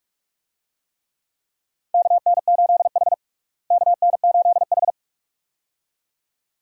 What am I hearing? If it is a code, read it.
Morse code, heard as a keyed tone, about 46 wpm, 700 Hz. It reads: KN8H KN8H
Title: USA Call Signs - 4 Characters - Repeated 2x - 45wpm